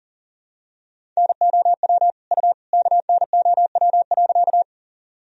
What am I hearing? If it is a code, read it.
Morse code, heard as a keyed tone, about 38 wpm, 700 Hz. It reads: NOW U KNOW.
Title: Sentences from Top 100 Words and Abbreviations 40wpm